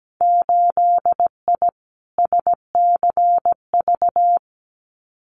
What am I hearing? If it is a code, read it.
Morse code, heard as a keyed tone, about 17 wpm, 700 Hz. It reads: 8I SCV
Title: Winter Field Day Exchanges Encore 17wpm